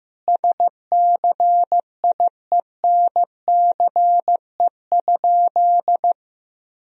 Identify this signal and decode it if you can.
Morse code, heard as a keyed tone, about 15 wpm, 700 Hz. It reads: SCIENCE?
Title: Sentences from Top 1000 Words - Review Part 2 - 15wpm